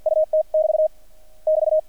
Morse code, heard as a keyed tone, about 50 wpm, 620 Hz. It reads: UT <BT> <BT>
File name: archived / CRSnet